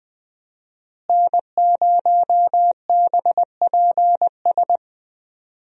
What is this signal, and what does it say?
Morse code, heard as a keyed tone, about 20 wpm, 700 Hz. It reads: N0BPS
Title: USA Call Signs - 5 Characters - Repeated 2x - 20wpm